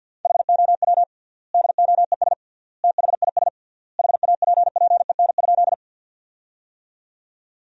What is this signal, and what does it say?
Morse code, heard as a keyed tone, about 50 wpm, 700 Hz. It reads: HOW DOES THIS HAPPEN?
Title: 3x Word-Spacing - Sentences from Top 400 Words - 50wpm